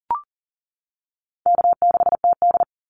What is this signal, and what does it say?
Morse code, heard as a keyed tone, about 40 wpm, 700 Hz. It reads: K6TD
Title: Easy Call Signs - 40wpm